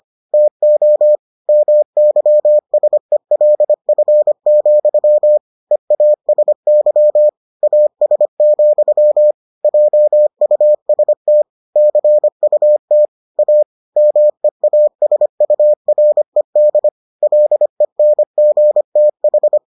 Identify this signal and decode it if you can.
Morse code, heard as a keyed tone, about 25 wpm, 600 Hz. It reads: TO MYSELF, EASY AS, JUST CUT A MEASURED LENGTH